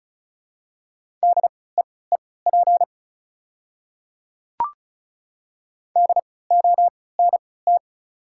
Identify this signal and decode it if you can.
Morse code, heard as a keyed tone, about 34 wpm, spaced as for 20 wpm, 700 Hz. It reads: DEEP DONT
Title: 3x Character Spacing - Top 500 Words - 35wpm